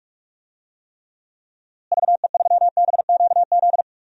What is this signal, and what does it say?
Morse code, heard as a keyed tone, about 46 wpm, 700 Hz. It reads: VE3BQZ
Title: DXCC-100 Call Signs - 6 Characters - Repeated 3x - 45wpm